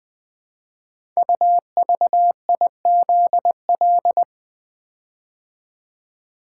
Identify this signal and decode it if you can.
Morse code, heard as a keyed tone, about 20 wpm, 700 Hz. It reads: UVIZL
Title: Code Groups - 5 Letters - 20wpm